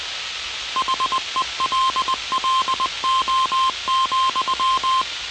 Morse code, heard as a keyed tone, about 20 wpm, 1050 Hz, with noise. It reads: HELLO,